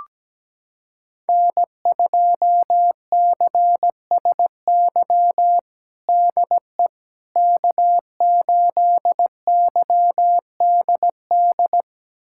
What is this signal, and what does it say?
Morse code, heard as a keyed tone, about 17 wpm, 700 Hz. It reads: N2CSY DE K8YDD
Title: USA Callsign DE Callsign - 5 Characters - 17wpm